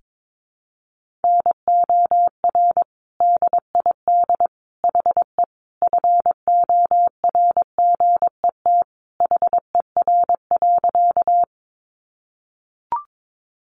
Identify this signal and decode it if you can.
Morse code, heard as a keyed tone, about 22 wpm, 700 Hz. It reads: NOR DID HE FORGET HER.
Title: Sentences from Top 1500 Words 22wpm